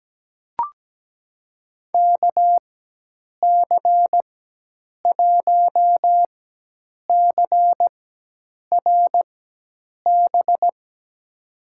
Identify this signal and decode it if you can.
Morse code, heard as a keyed tone, about 17 wpm, spaced as for 8 wpm, 700 Hz. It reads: KC1CRB
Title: USA Call Signs - 6 Characters - 4x Character Spacing - 17wpm